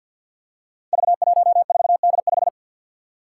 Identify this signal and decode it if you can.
Morse code, heard as a keyed tone, about 52 wpm, 700 Hz. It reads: V14D5
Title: Five Challenging Letter-Number Combinations 50wpm